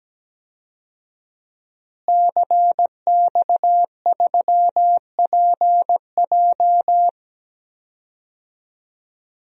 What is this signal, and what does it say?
Morse code, heard as a keyed tone, about 17 wpm, 700 Hz. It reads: CX3PJ
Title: DXCC-100 Call Signs - 5 Characters - Repeated 3x - 17wpm